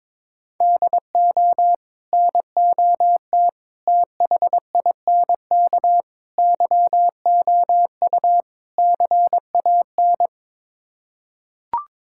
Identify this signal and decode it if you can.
Morse code, heard as a keyed tone, about 22 wpm, 700 Hz. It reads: DO NOT THINK YOU CAN
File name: practice / 22wpm / Build A Phrase - 5-Word Phrases - 22wpm